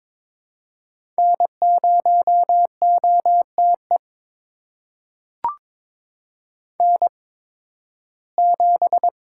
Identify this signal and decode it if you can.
Morse code, heard as a keyed tone, about 22 wpm, spaced as for 14 wpm, 700 Hz. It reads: N0OTE N7
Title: USA Call Signs - 8x Character Spacing - 22wpm